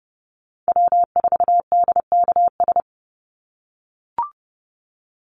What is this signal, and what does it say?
Morse code, heard as a keyed tone, about 28 wpm, 700 Hz. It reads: W4DKS
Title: Call Signs - 30wpm